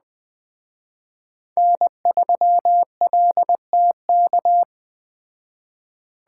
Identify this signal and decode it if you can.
Morse code, heard as a keyed tone, about 20 wpm, 700 Hz. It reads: N3LTK
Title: Speed Racing - USA 1x3 Call Signs 17wpm